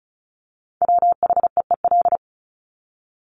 Morse code, heard as a keyed tone, about 34 wpm, 700 Hz. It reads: WHEEL
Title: Top 500 Words Repeated 35wpm